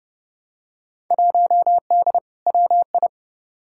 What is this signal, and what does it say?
Morse code, heard as a keyed tone, about 31 wpm, 700 Hz. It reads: 1D WI
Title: Field Day Exchanges - 30wpm